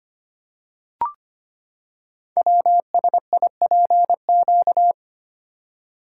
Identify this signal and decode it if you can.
Morse code, heard as a keyed tone, about 25 wpm, 700 Hz. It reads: WSIPQ
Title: Code Groups - 5 Letters - 25wpm